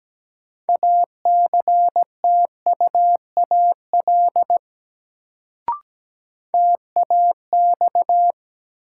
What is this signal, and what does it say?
Morse code, heard as a keyed tone, about 17 wpm, 700 Hz. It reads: ACTUAL TAX
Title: Sentences from Top 1000 Words - Review Part 5 - New Word Drill - 17wpm